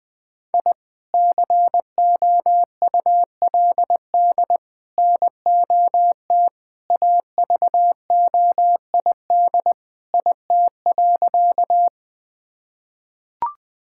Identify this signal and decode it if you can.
Morse code, heard as a keyed tone, about 20 wpm, 700 Hz. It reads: I COULD NOT AVOID IT.
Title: Sentences from Top 1000 Words - Review Part 6 - 20wpm